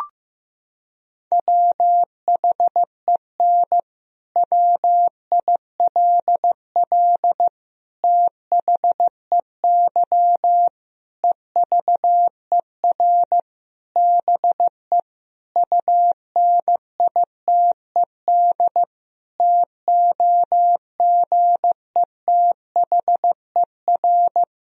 Morse code, heard as a keyed tone, about 14 wpm, 700 Hz. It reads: WHEN WILL THEY EVER BE UNITED TOGETHER
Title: Sentences from Top 1000 Words - Review Part 4 - 15wpm